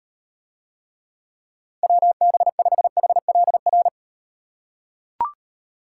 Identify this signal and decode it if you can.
Morse code, heard as a keyed tone, about 35 wpm, 700 Hz. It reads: WB5HLR